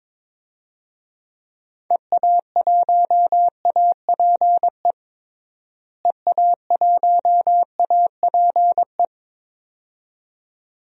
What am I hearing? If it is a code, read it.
Morse code, heard as a keyed tone, about 21 wpm, 700 Hz. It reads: EA1APE EA1APE
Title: DXCC-100 Call Signs - 6 Characters - Repeated 2x - 22wpm